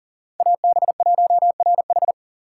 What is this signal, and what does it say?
Morse code, heard as a keyed tone, about 40 wpm, 700 Hz. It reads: AB1RH
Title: USA Call Signs - 3x Character Spacing - 40wpm